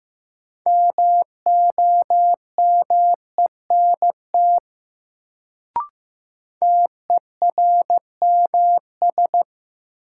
Morse code, heard as a keyed tone, about 15 wpm, 700 Hz. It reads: MOMENT TERMS
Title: Sets of 3 Words - New Word Drill - 15wpm